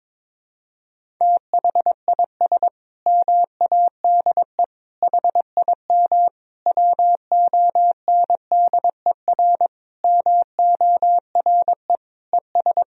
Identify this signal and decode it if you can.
Morse code, heard as a keyed tone, about 22 wpm, 700 Hz. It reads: THIS MADE HIM WONDER MORE ES
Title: Sentences from Top 500 Words and Abbreviations 22wpm